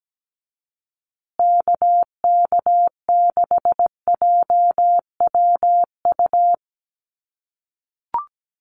Morse code, heard as a keyed tone, about 17 wpm, 700 Hz. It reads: KK6JWU
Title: USA Call Signs Repeated 3x 17wpm